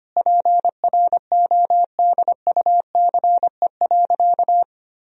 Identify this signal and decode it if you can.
Morse code, heard as a keyed tone, about 25 wpm, 700 Hz. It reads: PRODUCE.
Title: Sentences from Top 1000 Words - Review Part 3 - 25wpm